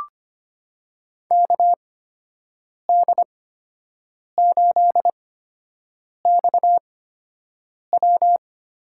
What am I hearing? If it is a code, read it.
Morse code, heard as a keyed tone, about 25 wpm, spaced as for 7 wpm, 700 Hz. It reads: KD8XW
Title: USA Call Signs - 5 Characters - 8x Character Spacing - 25wpm